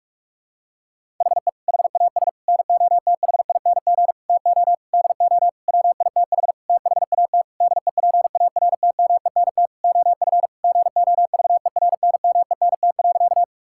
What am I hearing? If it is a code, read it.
Morse code, heard as a keyed tone, about 46 wpm, 700 Hz. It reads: HE HAS NOTHING TO DO WITH THAT DEPARTMENT OF GOVERNMENT.
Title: Sentences from Top 1000 Words - Review Part 5 - 45wpm